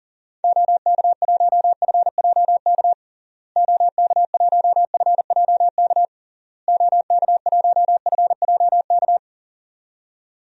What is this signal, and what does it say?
Morse code, heard as a keyed tone, about 40 wpm, 700 Hz. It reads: OK1FJK OK1FJK OK1FJK
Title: DXCC-100 Call Signs - 6 Characters - Repeated 3x - 40wpm